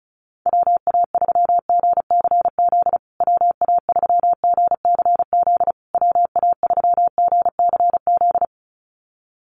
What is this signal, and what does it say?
Morse code, heard as a keyed tone, about 34 wpm, 700 Hz. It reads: WA3GCZ WA3GCZ WA3GCZ
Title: USA Call Signs Repeated 3x 35wpm